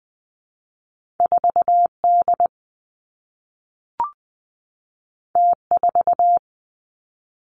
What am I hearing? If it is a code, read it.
Morse code, heard as a keyed tone, about 20 wpm, 700 Hz. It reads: 4 D T 4
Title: Two Challenging Letter-Number Combinations 20wpm